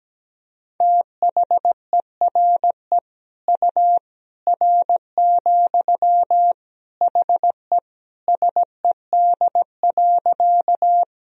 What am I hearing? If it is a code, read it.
Morse code, heard as a keyed tone, about 17 wpm, 700 Hz. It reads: THERE U R, HE SED.
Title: Sentences from Top 100 Words and Abbreviations 17wpm